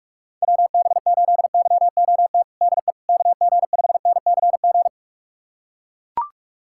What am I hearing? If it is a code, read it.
Morse code, heard as a keyed tone, about 42 wpm, 700 Hz. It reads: WB8YOT DE KG5NCG
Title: USA Callsign DE Callsign 45wpm